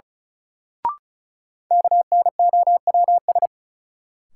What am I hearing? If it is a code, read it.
Morse code, heard as a keyed tone, about 36 wpm, 700 Hz. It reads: KNOWS
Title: Sentences from Top 1000 Words - Review Part 1 - New Word Drill - 35wpm